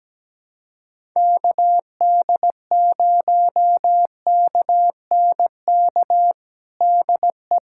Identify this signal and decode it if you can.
Morse code, heard as a keyed tone, about 17 wpm, 700 Hz. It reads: KD0KNK DE
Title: USA Callsign DE Callsign - 6 Characters - 17wpm